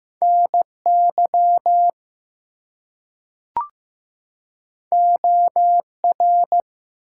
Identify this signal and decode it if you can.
Morse code, heard as a keyed tone, about 15 wpm, 700 Hz. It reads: NY OR
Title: US States and Canadian Provinces 15wpm